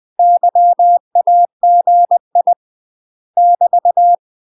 Morse code, heard as a keyed tone, about 20 wpm, 700 Hz. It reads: YAGI <BT>
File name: UK MASTER QSOs TRAINING/20WPM